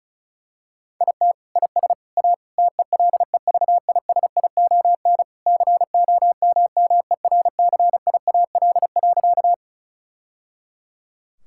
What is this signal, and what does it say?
Morse code, heard as a keyed tone, about 36 wpm, 700 Hz. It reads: IT IS A TELEVISION COMMERCIAL.
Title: Sentences from Top 1000 Words - Review Part 5 - 35wpm